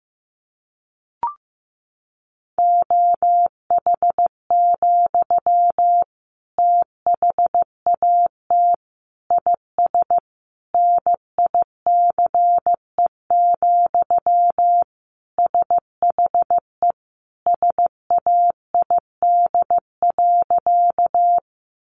Morse code, heard as a keyed tone, about 15 wpm, 700 Hz. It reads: OH, THAT IS NICE, SHE SAID.
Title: Sentences from Top 1100 Words 15wpm